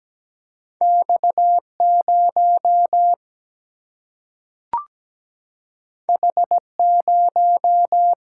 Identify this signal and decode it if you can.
Morse code, heard as a keyed tone, about 17 wpm, 700 Hz. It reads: X0 H0
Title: Single-Letter Single-Number 17wpm